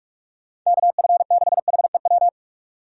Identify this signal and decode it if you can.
Morse code, heard as a keyed tone, about 46 wpm, 700 Hz. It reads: KF6HEW
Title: USA Call Signs - 3x Character Spacing - 45wpm